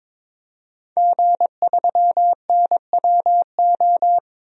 Morse code, heard as a keyed tone, about 22 wpm, 700 Hz. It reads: G3NWO
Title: DXCC-100 Call Signs - 5 Characters - 22wpm